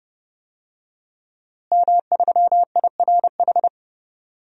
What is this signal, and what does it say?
Morse code, heard as a keyed tone, about 30 wpm, 700 Hz. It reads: M3IRH
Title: DXCC-100 Call Signs - 5 Characters - Repeated 2x - 30wpm